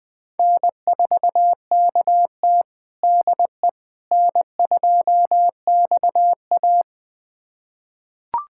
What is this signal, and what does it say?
Morse code, heard as a keyed tone, about 20 wpm, 700 Hz. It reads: N4KT DE N2XA
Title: USA Callsign DE Callsign - 4 Characters - 20wpm